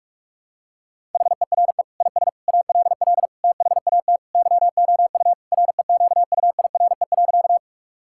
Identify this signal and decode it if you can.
Morse code, heard as a keyed tone, about 46 wpm, 700 Hz. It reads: HERE IS ALL THAT YOU REQUIRE.
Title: Sentences from Top 1000 Words - Review Part 6 - 45wpm